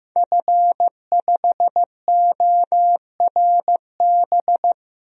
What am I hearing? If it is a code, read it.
Morse code, heard as a keyed tone, about 15 wpm, 700 Hz. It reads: F5ORB
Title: DXCC-100 Call Signs - 5 Characters - Repeated 2x - 15wpm